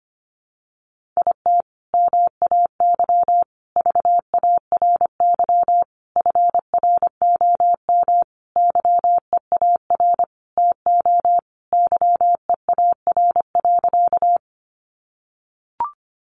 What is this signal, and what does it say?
Morse code, heard as a keyed tone, about 25 wpm, 700 Hz. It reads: IT MAY VARY FROM YEAR TO YEAR.
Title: Sentences from Top 1200 Words 25wpm